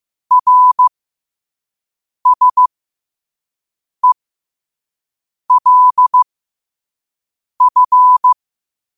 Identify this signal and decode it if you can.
Morse code, heard as a keyed tone, about 15 wpm, spaced as for 5 wpm, 1000 Hz. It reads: RSELF
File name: 5wpm